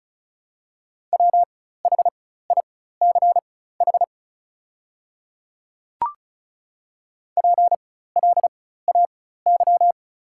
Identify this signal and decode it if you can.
Morse code, heard as a keyed tone, about 36 wpm, spaced as for 16 wpm, 700 Hz. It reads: WHICH PLAY
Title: 4x Character Spacing - Top 500 Words - 35wpm